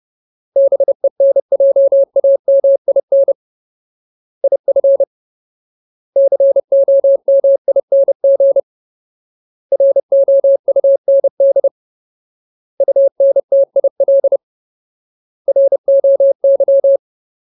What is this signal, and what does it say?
Morse code, heard as a keyed tone, about 30 wpm, 550 Hz. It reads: BENJAMIN IF COMING ROUND UNTIL ROY
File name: Ultimate Word List / 30WPM Ultimate